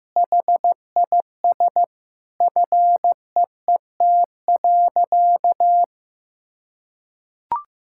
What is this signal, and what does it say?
Morse code, heard as a keyed tone, about 15 wpm, 700 Hz. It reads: HIS FEET.
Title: Sentences from Top 1000 Words - Review Part 6 - 15wpm